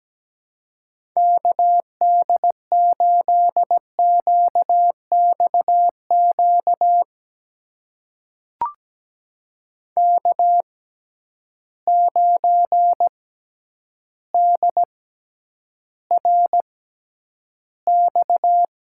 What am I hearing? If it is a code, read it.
Morse code, heard as a keyed tone, about 17 wpm, spaced as for 9 wpm, 700 Hz. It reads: KD8QXQ K9DRX
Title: USA Call Signs - 6x Character Spacing - 17wpm